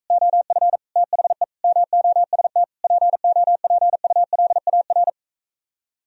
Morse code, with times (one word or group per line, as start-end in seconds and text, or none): 0.1-0.8 OF
1.0-1.4 THE
1.6-2.6 MOST
2.8-5.1 POPULAR